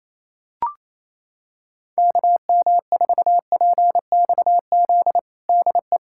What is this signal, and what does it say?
Morse code, heard as a keyed tone, about 28 wpm, 700 Hz. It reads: KM4PXZ DE
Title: USA Callsign DE Callsign - 6 Characters - 28wpm